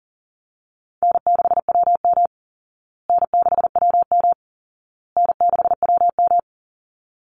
Morse code, heard as a keyed tone, about 40 wpm, 700 Hz. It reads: N6WM N6WM N6WM
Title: Call Signs Repeated 2x - 40wpm